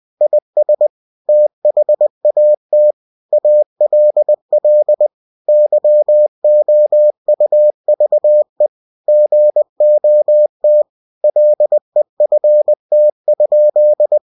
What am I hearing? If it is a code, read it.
Morse code, heard as a keyed tone, about 20 wpm, 600 Hz. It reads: IS THAT ALL YOUVE GOT LEFT?
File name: cw-books / winnie-the-pooh / winnie-20wpm-page-0038